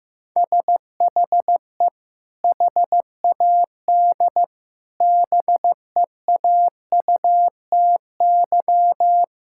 Morse code, heard as a keyed tone, about 15 wpm, 700 Hz. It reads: SHE HAD BEAUTY
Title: Sentences from Top 1000 Words - Review Part 5 - 15wpm